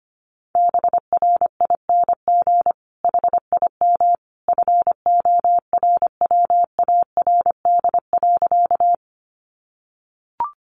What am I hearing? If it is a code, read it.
Morse code, heard as a keyed tone, about 25 wpm, 700 Hz. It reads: BRING HIM FORWARD.
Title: Sentences from Top 1300 Words 25wpm